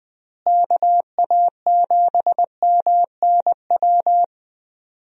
Morse code, heard as a keyed tone, about 21 wpm, 700 Hz. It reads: KA7MNW
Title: USA Call Signs - 6 Characters - Repeated 3x - 20wpm